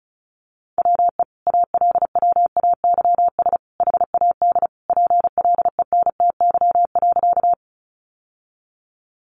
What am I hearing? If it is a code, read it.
Morse code, heard as a keyed tone, about 34 wpm, 700 Hz. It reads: WE ALWAYS HAD PLENTY.
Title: Sentences from Top 1300 Words 35wpm